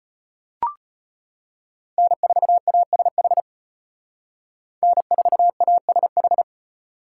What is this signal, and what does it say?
Morse code, heard as a keyed tone, about 33 wpm, 700 Hz. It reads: N4ASH N4ASH